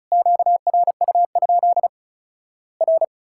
Morse code, heard as a keyed tone, about 36 wpm, 700 Hz. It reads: QRU? R
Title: Worked All Europe DX Contest - QTC Exchange - 35wpm